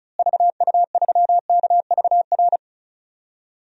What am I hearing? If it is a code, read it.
Morse code, heard as a keyed tone, about 34 wpm, 700 Hz. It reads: VU3KVR
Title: DXCC-100 Call Signs - 6 Characters - Repeated 3x - 35wpm